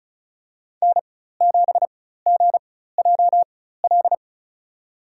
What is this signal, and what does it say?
Morse code, heard as a keyed tone, about 34 wpm, spaced as for 16 wpm, 700 Hz. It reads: N7GJL
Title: USA Call Signs - 5 Characters - 4x Character Spacing - 35wpm